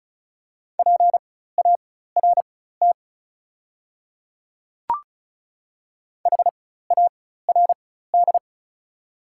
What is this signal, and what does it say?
Morse code, heard as a keyed tone, about 36 wpm, spaced as for 16 wpm, 690 Hz. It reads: PART HARD
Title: 4x Character Spacing - Top 300 Words - 35wpm